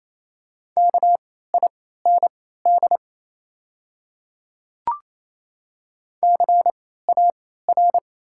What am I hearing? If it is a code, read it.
Morse code, heard as a keyed tone, about 27 wpm, spaced as for 16 wpm, 690 Hz. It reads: KIND CAR
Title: 3x Character Spacing - Top 300 Words - 28wpm